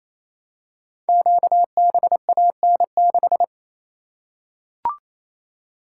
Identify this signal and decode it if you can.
Morse code, heard as a keyed tone, about 29 wpm, 700 Hz. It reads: QBAN6
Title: Code Groups - 5 Letter:Numbers - 28wpm